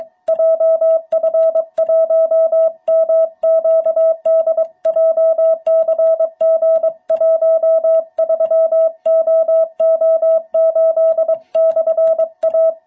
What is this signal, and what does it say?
Morse code, heard as a keyed tone, about 23 wpm, 660 Hz. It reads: JF1MQD JCG13OO8/A
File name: jh8cbh / 2021cw